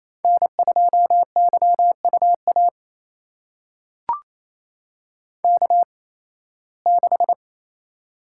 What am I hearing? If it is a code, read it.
Morse code, heard as a keyed tone, about 28 wpm, spaced as for 18 wpm, 700 Hz. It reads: N2YUA K6
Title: USA Call Signs - 5 Characters - 8x Character Spacing - 28wpm